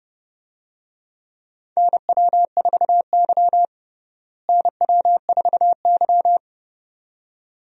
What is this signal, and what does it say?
Morse code, heard as a keyed tone, about 30 wpm, 700 Hz. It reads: NW4Y NW4Y
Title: USA Call Signs - 4 Characters - Repeated 2x - 30wpm